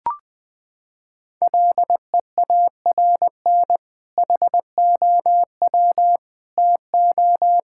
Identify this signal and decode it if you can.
Morse code, heard as a keyed tone, about 20 wpm, 700 Hz. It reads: LEARN HOW TO
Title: Sentences with 2-Word Phrases - 20wpm